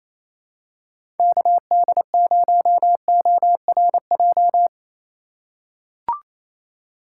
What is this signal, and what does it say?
Morse code, heard as a keyed tone, about 28 wpm, 700 Hz. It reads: KD0ORJ
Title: USA Call Signs - 6 Characters - 4x Character Spacing - 28wpm